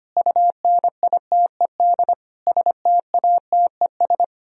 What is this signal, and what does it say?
Morse code, heard as a keyed tone, about 31 wpm, 700 Hz. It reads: UNITED STATES
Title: Speed-Racing - Sets of 5 Words 25wpm